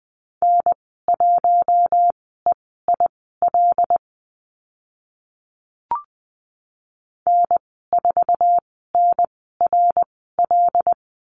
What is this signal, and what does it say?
Morse code, heard as a keyed tone, about 20 wpm, spaced as for 15 wpm, 700 Hz. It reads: N1EIL N4NRL
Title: 082 Lesson 14g - Number 2 - Call Signs 20wpm